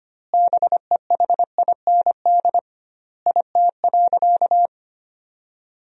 Morse code, heard as a keyed tone, about 25 wpm, 700 Hz. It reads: BEHIND IT.
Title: 2x Word-Spacing - Sentences from Top 500 Words - 25wpm